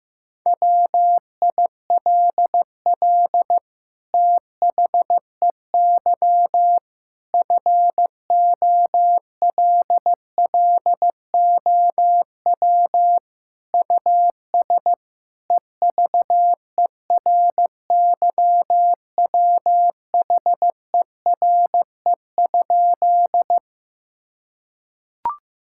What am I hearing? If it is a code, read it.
Morse code, heard as a keyed tone, about 16 wpm, 700 Hz. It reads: WILL THEY FOLLOW US EVERYWHERE?
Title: Sentences from Top 1600 Words 15wpm